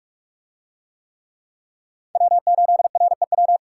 Morse code, heard as a keyed tone, about 44 wpm, 700 Hz. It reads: W8REW